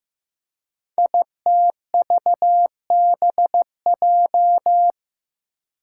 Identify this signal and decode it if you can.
Morse code, heard as a keyed tone, about 15 wpm, 700 Hz. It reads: ITVBJ
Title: Code Groups - 5 Letters - 15wpm